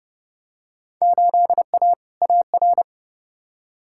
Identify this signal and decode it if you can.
Morse code, heard as a keyed tone, about 32 wpm, 700 Hz. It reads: 8A AR
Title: Field Day Exchanges - 30wpm